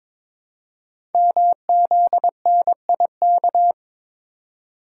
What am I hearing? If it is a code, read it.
Morse code, heard as a keyed tone, about 22 wpm, 700 Hz. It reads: MZNIK
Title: Code Groups - 5 Letters - 22wpm